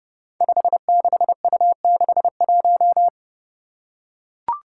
Morse code, heard as a keyed tone, about 31 wpm, 700 Hz. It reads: 56U61
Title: Five Challenging Letter-Number Combinations 30wpm